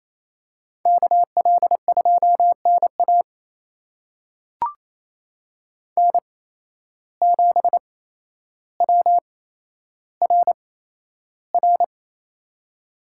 Morse code, heard as a keyed tone, about 28 wpm, spaced as for 12 wpm, 700 Hz. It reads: KL2NA N7WRR
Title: USA Call Signs - 5 Characters - 8x Character Spacing - 28wpm